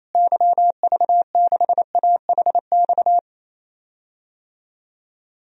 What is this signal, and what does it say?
Morse code, heard as a keyed tone, about 28 wpm, 700 Hz. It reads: YV6AHX
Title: DXCC-100 Call Signs - 6 Characters - Repeated 3x - 28wpm